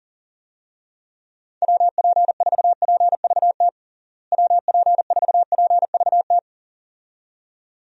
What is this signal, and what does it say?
Morse code, heard as a keyed tone, about 37 wpm, 700 Hz. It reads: WP4PVT WP4PVT
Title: USA Call Signs - 6 Characters - Repeated 2x - 40wpm